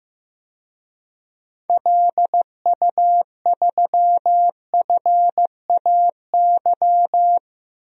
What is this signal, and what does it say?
Morse code, heard as a keyed tone, about 15 wpm, 700 Hz. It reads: LU3FAY
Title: DXCC-100 Call Signs - 6 Characters - 15wpm